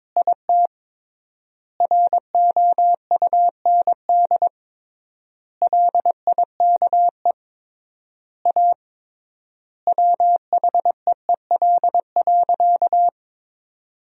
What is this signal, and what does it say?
Morse code, heard as a keyed tone, about 22 wpm, 700 Hz. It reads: IT ROUND LIKE A WHEEL.
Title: 3x Word-Spacing - Sentences from Top 500 Words - 22wpm